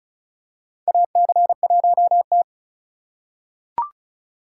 Morse code, heard as a keyed tone, about 36 wpm, 700 Hz. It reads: AC1T
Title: USA Call Signs - 4 Characters - 2x Character Spacing - 35wpm